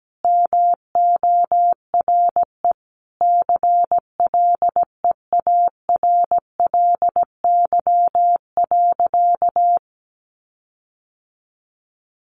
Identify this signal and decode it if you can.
Morse code, heard as a keyed tone, about 17 wpm, 700 Hz. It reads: MORE CLEARLY.
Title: Sentences from Top 1300 Words 17wpm